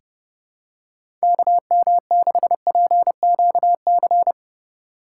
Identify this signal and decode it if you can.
Morse code, heard as a keyed tone, about 31 wpm, 700 Hz. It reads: KM6PQC
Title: USA Call Signs - 6 Characters - 8x Character Spacing - 30wpm